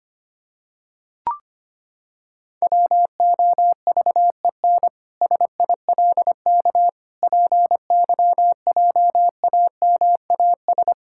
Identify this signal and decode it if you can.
Morse code, heard as a keyed tone, about 25 wpm, 700 Hz. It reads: WOVEN SILK PYJAMAS
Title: Pangram Challenge v2 - 25wpm